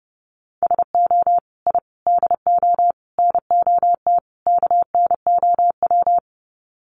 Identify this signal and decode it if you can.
Morse code, heard as a keyed tone, about 30 wpm, 700 Hz. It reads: SO I DO NOT KNOW
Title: Sets of 5 Words Encore 30wpm